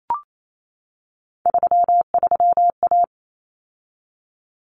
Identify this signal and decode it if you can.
Morse code, heard as a keyed tone, about 28 wpm, 700 Hz. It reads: 33A
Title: Common Cut Numbers - 3-Digit 28wpm